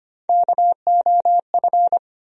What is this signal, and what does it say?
Morse code, heard as a keyed tone, about 25 wpm, 700 Hz. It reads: KOF
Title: USA Call Sign Postfixes - 3 Letters 25wpm